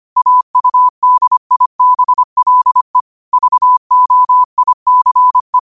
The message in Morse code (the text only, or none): AUDIBLE VOICE